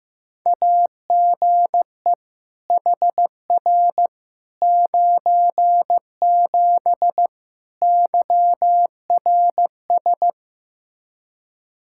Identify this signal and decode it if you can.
Morse code, heard as a keyed tone, about 15 wpm, 700 Hz. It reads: AGE HR 97 YRS
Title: Practice QSO - Exchange 3 - More Info - 15wpm